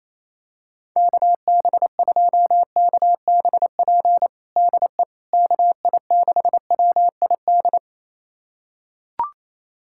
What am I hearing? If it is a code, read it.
Morse code, heard as a keyed tone, about 27 wpm, 700 Hz. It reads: KB2KBP DE KI6WID
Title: USA Callsign DE Callsign - 6 Characters - 28wpm